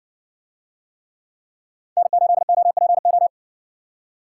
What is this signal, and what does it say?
Morse code, heard as a keyed tone, about 60 wpm, 700 Hz. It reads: N8OPO